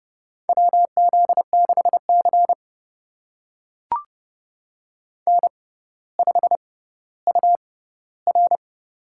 Morse code, heard as a keyed tone, about 32 wpm, spaced as for 16 wpm, 700 Hz. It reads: WZ6C N5UR
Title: USA Call Signs - 4 Characters - 6x Character Spacing - 30wpm